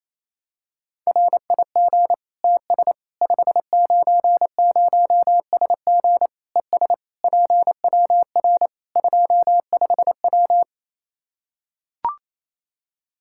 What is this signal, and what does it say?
Morse code, heard as a keyed tone, about 28 wpm, 700 Hz. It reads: RIG TS 590SG ES PWR 25W
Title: K1USN SST QSO Practice 28wpm